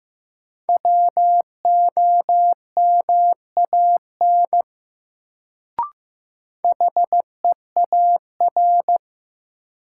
Morse code, heard as a keyed tone, about 16 wpm, 700 Hz. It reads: WOMAN HEAR
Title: Sets of 5 Words - New Word Drill - 15wpm